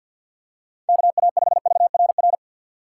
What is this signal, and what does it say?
Morse code, heard as a keyed tone, about 47 wpm, 700 Hz. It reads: KA5VRR
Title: USA Call Signs - 6 Characters - 3x Character Spacing - 50wpm